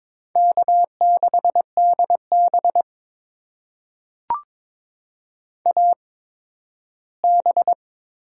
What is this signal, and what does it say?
Morse code, heard as a keyed tone, about 22 wpm, spaced as for 13 wpm, 700 Hz. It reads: K6DB AB
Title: USA Call Signs - 4 Characters - 8x Character Spacing - 22wpm